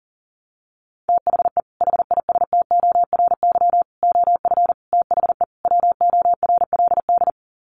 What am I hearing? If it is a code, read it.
Morse code, heard as a keyed tone, about 40 wpm, 700 Hz. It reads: THE HISTORY OF THE WORLD
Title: Sets of 5 Words Encore2 40wpm